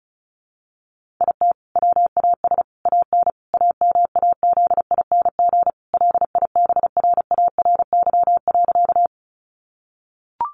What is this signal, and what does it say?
Morse code, heard as a keyed tone, about 34 wpm, 700 Hz. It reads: IT WAS AN AMAZING LIBRARY.
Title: Sentences from Top 1500 Words 35wpm